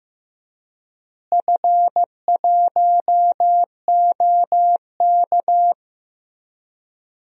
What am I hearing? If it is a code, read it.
Morse code, heard as a keyed tone, about 15 wpm, 700 Hz. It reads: F1OK
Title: DXCC-100 Call Signs - 4 Characters - Repeated 3x - 15wpm